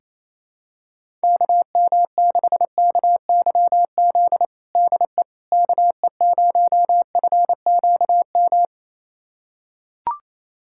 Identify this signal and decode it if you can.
Morse code, heard as a keyed tone, about 28 wpm, 700 Hz. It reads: KM6KYZ DE KE0FQM
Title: USA Callsign DE Callsign 28wpm